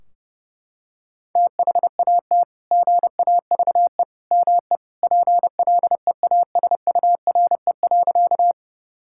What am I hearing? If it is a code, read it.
Morse code, heard as a keyed tone, about 28 wpm, 700 Hz. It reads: THAT GAVE ME PLEASURE.
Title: Sentences from Top 1600 Words 30wpm